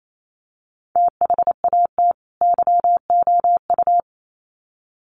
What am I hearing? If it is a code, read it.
Morse code, heard as a keyed tone, about 28 wpm, 700 Hz. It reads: THAT YOU
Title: Sets of 2 Words 28wpm